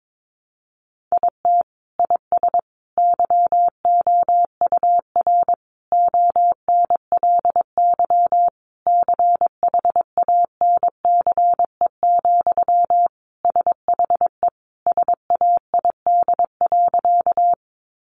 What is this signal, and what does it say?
Morse code, heard as a keyed tone, about 22 wpm, 700 Hz. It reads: IT IS YOUR ONLY CHANCE, SHE SAID.
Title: Sentences from Top 800 Words 22wpm